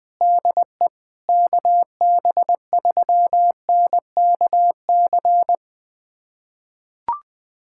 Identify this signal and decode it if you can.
Morse code, heard as a keyed tone, about 20 wpm, 700 Hz. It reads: DE KB3NKC
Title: USA Callsign DE Callsign - 6 Characters - 20wpm